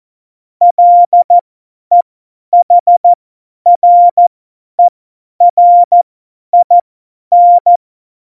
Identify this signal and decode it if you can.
Morse code, heard as a keyed tone, about 14 wpm, spaced as for 10 wpm, 700 Hz. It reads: LEHRERIN
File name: Morsekurs / Witze / Witz 26. November 2019, 50 F70